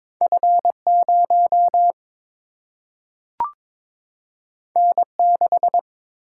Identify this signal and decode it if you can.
Morse code, heard as a keyed tone, about 22 wpm, 700 Hz. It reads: F 0 N 6
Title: Single-Letter Single-Number 22wpm